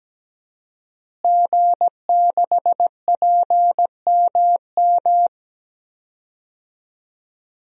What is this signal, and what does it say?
Morse code, heard as a keyed tone, about 17 wpm, 700 Hz. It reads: G6PMM
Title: DXCC-100 Call Signs - 5 Characters - Repeated 2x - 17wpm